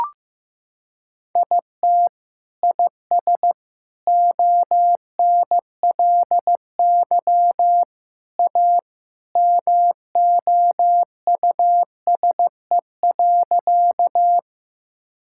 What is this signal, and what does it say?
Morse code, heard as a keyed tone, about 15 wpm, 700 Hz. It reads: IT IS ONLY A MOUSE.
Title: Sentences from Top 1000 Words - Review Part 5 - 15wpm